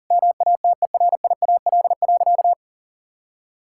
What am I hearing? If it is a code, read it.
Morse code, heard as a keyed tone, about 40 wpm, 700 Hz. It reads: MATERIAL.
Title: Sentences from Top 1000 Words - Review Part 5 - 40wpm